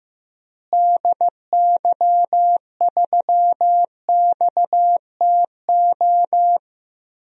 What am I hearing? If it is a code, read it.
Morse code, heard as a keyed tone, about 15 wpm, 700 Hz. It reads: DY3XTO
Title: DXCC-100 Call Signs - 6 Characters - Repeated 3x - 15wpm